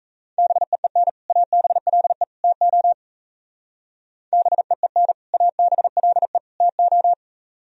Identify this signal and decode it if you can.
Morse code, heard as a keyed tone, about 36 wpm, 700 Hz. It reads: BEEN ABLE TO BEEN ABLE TO